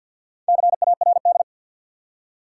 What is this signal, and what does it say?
Morse code, heard as a keyed tone, about 52 wpm, 700 Hz. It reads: CARD
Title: Four Letter Words - 50wpm